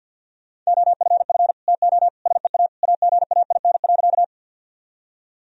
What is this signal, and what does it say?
Morse code, heard as a keyed tone, about 52 wpm, 700 Hz. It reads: OFF TO SEA AGAIN.
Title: Sentences from Top 1000 Words - Review Part 1 - 50wpm